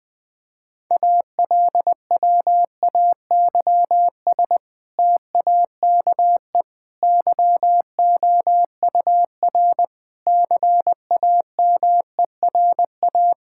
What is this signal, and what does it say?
Morse code, heard as a keyed tone, about 20 wpm, 700 Hz. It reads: ALWAYS TAKE YOUR CAMERA
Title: Sentences from Top 1000 Words - Review Part 5 - 20wpm